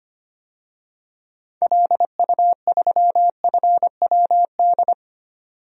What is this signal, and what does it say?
Morse code, heard as a keyed tone, about 25 wpm, 700 Hz. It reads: LU3FWD
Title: DXCC-100 Call Signs - 6 Characters - 25wpm